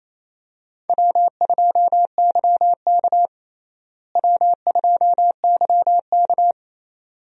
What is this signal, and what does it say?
Morse code, heard as a keyed tone, about 27 wpm, 700 Hz. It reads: W2YK W2YK
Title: USA Call Signs - 4 Characters - Repeated 2x - 28wpm